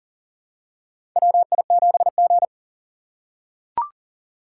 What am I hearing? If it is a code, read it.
Morse code, heard as a keyed tone, about 40 wpm, 700 Hz. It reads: WI7G
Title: USA Call Signs - 4 Characters - 4x Character Spacing - 40wpm